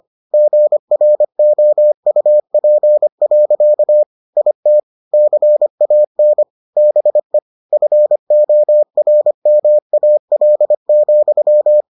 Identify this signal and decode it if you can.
Morse code, heard as a keyed tone, about 25 wpm, 600 Hz. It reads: GROUP. IT CAN BE FORMAL,